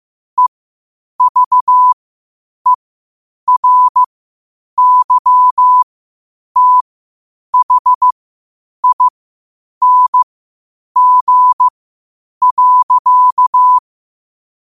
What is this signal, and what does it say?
Morse code, heard as a keyed tone, about 15 wpm, spaced as for 8 wpm, 1000 Hz. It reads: EVERYTHING.
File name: qotdfiles / 8wpm